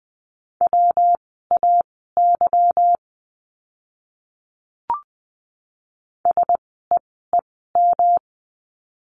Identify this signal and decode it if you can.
Morse code, heard as a keyed tone, about 20 wpm, spaced as for 15 wpm, 700 Hz. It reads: WAY SEEM
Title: 230 Lesson 37f - Letter x - Words 20wpm